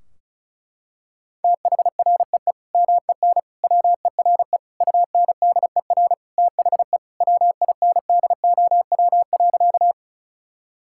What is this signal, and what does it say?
Morse code, heard as a keyed tone, about 34 wpm, 700 Hz. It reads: THREE MEN WERE UNDER THE WINDOW.
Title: Sentences from Top 1600 Words 35wpm